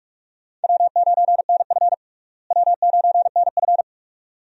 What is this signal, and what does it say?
Morse code, heard as a keyed tone, about 46 wpm, 700 Hz. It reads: W9NF W9NF
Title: USA Call Signs - 4 Characters - Repeated 2x - 45wpm